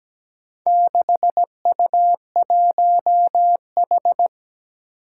Morse code, heard as a keyed tone, about 17 wpm, 700 Hz. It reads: 6U1H
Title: Four Challenging Letter-Number Combinations 17wpm